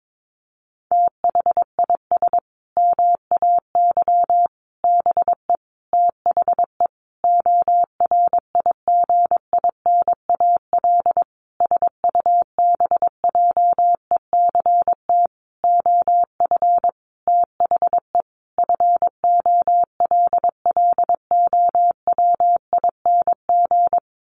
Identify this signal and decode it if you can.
Morse code, heard as a keyed tone, about 22 wpm, 700 Hz. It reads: THIS MAY BE THE ORIGINAL SUBJECT OF THE FOLLOWING
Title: Sentences from Top 900 Words 22wpm